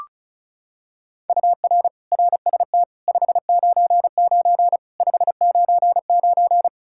Morse code, heard as a keyed tone, about 34 wpm, 700 Hz. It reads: UR RST 599 599
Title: K1USN SST QSO Practice 35wpm